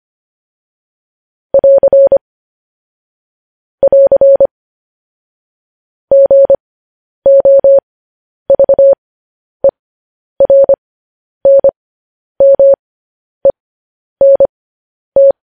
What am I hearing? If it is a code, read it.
Morse code, heard as a keyed tone, about 25 wpm, spaced as for 10 wpm, 550 Hz. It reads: <AR> <AR> GOVERNMENT
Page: News Headlines in Morse Code at 10 WPM